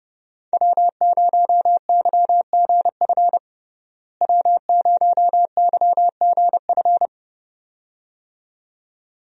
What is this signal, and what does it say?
Morse code, heard as a keyed tone, about 30 wpm, 700 Hz. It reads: W0YGF W0YGF
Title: USA Call Signs - 5 Characters - Repeated 2x - 30wpm